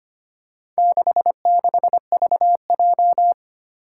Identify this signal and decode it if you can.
Morse code, heard as a keyed tone, about 25 wpm, 700 Hz. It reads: 66VJ
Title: Four Challenging Letter-Number Combinations 25wpm